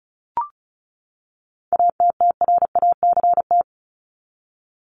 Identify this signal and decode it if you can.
Morse code, heard as a keyed tone, about 34 wpm, 700 Hz. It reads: ATTRACT
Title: Top 1501-1750 Words 35wpm